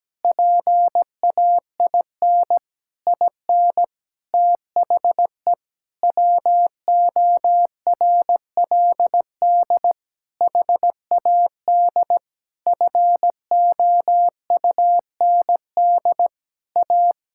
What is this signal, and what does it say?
Morse code, heard as a keyed tone, about 17 wpm, 700 Hz. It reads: PAIN IN THE WORLD HAD FOUND A
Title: Sentences from Top 1600 Words 17wpm